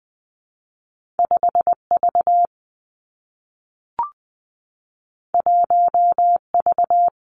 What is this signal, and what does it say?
Morse code, heard as a keyed tone, about 20 wpm, 700 Hz. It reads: 5 V 1 V
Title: Two Challenging Letter-Number Combinations 20wpm